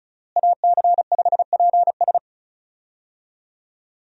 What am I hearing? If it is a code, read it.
Morse code, heard as a keyed tone, about 36 wpm, 700 Hz. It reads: AC5PS
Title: USA Call Signs - 5 Characters - Repeated 3x - 35wpm